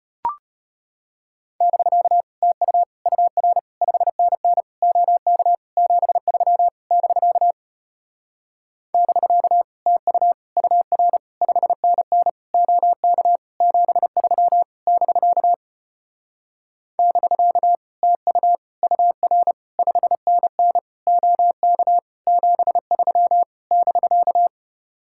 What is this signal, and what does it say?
Morse code, heard as a keyed tone, about 33 wpm, 700 Hz. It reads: <BK> TU UR 5NN OK 73 <BK> <BK> TU UR 5NN OK 73 <BK> <BK> TU UR 5NN OK 73 <BK>